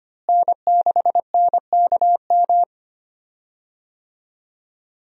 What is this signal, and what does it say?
Morse code, heard as a keyed tone, about 25 wpm, 700 Hz. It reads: N6NKM
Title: USA Call Signs - 5 Characters - Repeated 2x - 25wpm